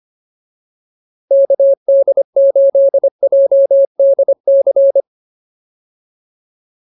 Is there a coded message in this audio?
Yes: Morse code from 1.3 to 5.0 s, heard as a keyed tone, about 25 wpm, 550 Hz. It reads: KD8JDC